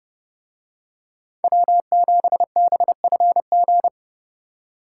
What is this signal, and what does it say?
Morse code, heard as a keyed tone, about 30 wpm, 700 Hz. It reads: W7BFG